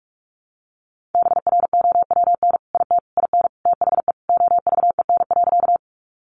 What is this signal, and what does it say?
Morse code, heard as a keyed tone, about 44 wpm, 700 Hz. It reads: BROWN IT IN THE OVEN.
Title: Sentences from Top 1200 Words 45wpm